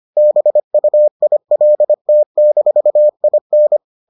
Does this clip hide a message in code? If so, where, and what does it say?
Yes: Morse code from 0.2 to 3.8 s, heard as a keyed tone, about 25 wpm, 600 Hz. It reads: BUILT-IN